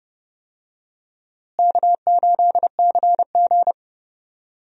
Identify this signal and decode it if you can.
Morse code, heard as a keyed tone, about 31 wpm, 700 Hz. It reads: K8CG
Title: USA Call Signs - 4 Characters - Repeated 3x - 30wpm